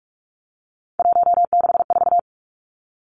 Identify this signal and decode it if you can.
Morse code, heard as a keyed tone, about 44 wpm, 700 Hz. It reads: 164
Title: Numbers - 3-Digit 45wpm